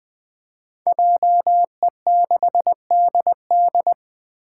Morse code, heard as a keyed tone, about 20 wpm, 700 Hz. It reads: JE6DD
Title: Five Challenging Letter-Number Combinations 20wpm